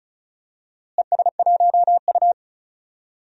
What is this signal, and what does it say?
Morse code, heard as a keyed tone, about 34 wpm, 700 Hz. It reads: ES1U
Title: DXCC-100 Call Signs - 4 Characters - Repeated 3x - 35wpm